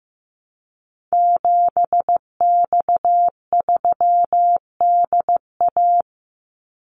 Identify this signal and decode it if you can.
Morse code, heard as a keyed tone, about 15 wpm, 700 Hz. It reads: 7X3DA
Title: Call Signs Repeated 3x - 15wpm